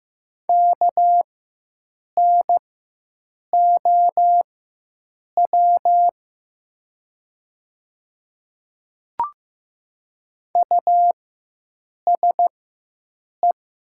Morse code, heard as a keyed tone, about 15 wpm, spaced as for 7 wpm, 700 Hz. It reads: KNOW USE
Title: 4x Character Spacing - Top 100 Words - 15wpm